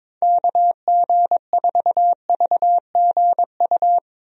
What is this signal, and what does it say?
Morse code, heard as a keyed tone, about 22 wpm, 700 Hz. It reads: KG4VGU
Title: USA Call Signs - 6 Characters - Repeated 2x - 22wpm